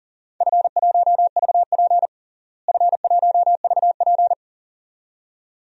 Morse code, heard as a keyed tone, about 37 wpm, 700 Hz. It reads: F1VP F1VP
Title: DXCC-100 Call Signs - 4 Characters - Repeated 2x - 40wpm